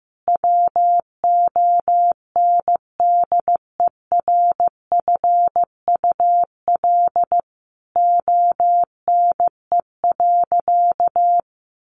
Morse code, heard as a keyed tone, about 15 wpm, 700 Hz. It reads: WONDERFUL ONE.
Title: Sentences from Top 1100 Words 15wpm